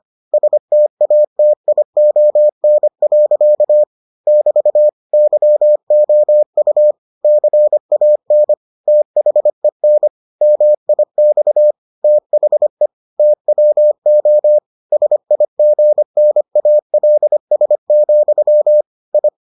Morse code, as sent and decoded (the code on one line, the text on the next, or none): ... - .- - .. --- -. .-.-.- / -...- / -.-- --- ..- / -.-. .- -. / - .... -. / -- .. -..- / - .... / - .-- --- / ... .. --. -. .- .-.. ... --..-- / ..
STATION. <BT> YOU CAN THEN MIX THE TWO SIGNALS, I